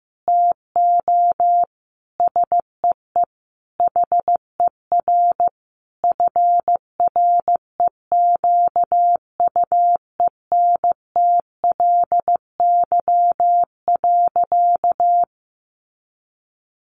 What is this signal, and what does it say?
Morse code, heard as a keyed tone, about 15 wpm, 700 Hz. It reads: TO SEE HER FREQUENTLY.
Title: Sentences from Top 1300 Words 15wpm